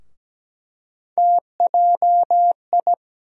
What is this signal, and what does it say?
Morse code, heard as a keyed tone, about 17 wpm, 700 Hz. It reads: TJI
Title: USA Call Sign Postfixes - 3 Letters 17wpm